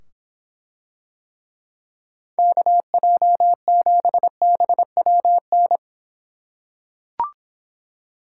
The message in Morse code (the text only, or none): KJ7BWN